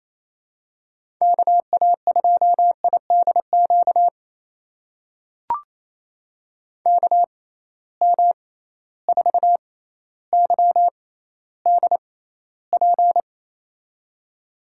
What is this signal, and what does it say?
Morse code, heard as a keyed tone, about 28 wpm, spaced as for 14 wpm, 700 Hz. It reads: KA2IDQ KM4YDP
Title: USA Call Signs - 6 Characters - 6x Character Spacing - 28wpm